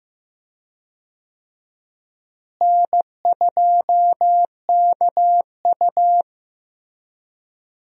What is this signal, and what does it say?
Morse code, heard as a keyed tone, about 15 wpm, 700 Hz. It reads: N2KU
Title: USA Call Signs - 4 Characters - Repeated 2x - 15wpm